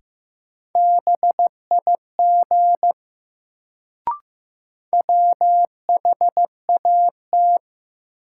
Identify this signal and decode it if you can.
Morse code, heard as a keyed tone, about 15 wpm, 700 Hz. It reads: BIG WHAT
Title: Sets of 2 Words - New Word Drill - 15wpm